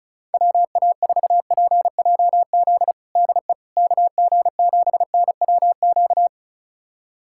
Morse code, heard as a keyed tone, about 36 wpm, 700 Hz. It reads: WA4PJZ DE KG7NWQ
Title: USA Callsign DE Callsign 35wpm